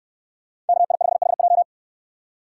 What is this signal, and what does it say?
Morse code, heard as a keyed tone, about 69 wpm, 700 Hz. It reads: KI5HJ